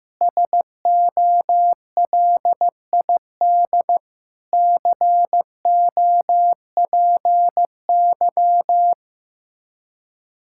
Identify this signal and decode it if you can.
Morse code, heard as a keyed tone, about 15 wpm, 700 Hz. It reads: SOLID COPY
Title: Practice QSO - Exchange 3 - More Info - 15wpm